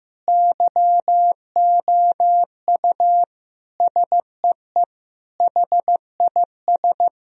Morse code, heard as a keyed tone, about 15 wpm, 700 Hz. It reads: YOU SEE HIS
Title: Sentences from Top 1000 Words - Review Part 2 - 15wpm